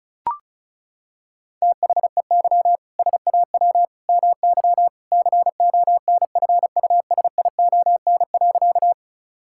Morse code, heard as a keyed tone, about 34 wpm, 700 Hz. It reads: THEY SAW MY CONFUSION.
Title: Sentences from Top 1600 Words 35wpm